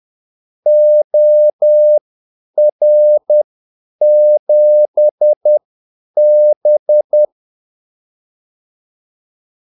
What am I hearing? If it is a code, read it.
Morse code, heard as a keyed tone, about 10 wpm, spaced as for 8 wpm, 600 Hz. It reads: OR7B
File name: Callsigns-10-8wpm